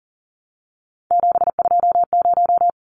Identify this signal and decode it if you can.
Morse code, heard as a keyed tone, about 40 wpm, 700 Hz. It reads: 720
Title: Numbers - 3-Digit 40wpm